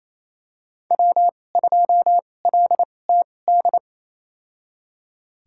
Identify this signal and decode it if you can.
Morse code, heard as a keyed tone, about 28 wpm, spaced as for 20 wpm, 700 Hz. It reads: W2LTD
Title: USA Call Signs - 5 Characters - 2x Character Spacing - 28wpm